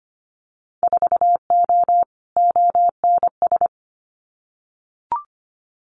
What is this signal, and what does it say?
Morse code, heard as a keyed tone, about 25 wpm, 700 Hz. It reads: 4O ONS
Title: Winter Field Day Exchanges 25wpm